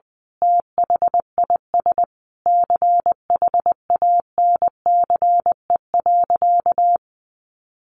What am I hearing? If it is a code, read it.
Morse code, heard as a keyed tone, about 20 wpm, 700 Hz. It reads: THIS CHANCE.
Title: Sentences from Top 800 Words 20wpm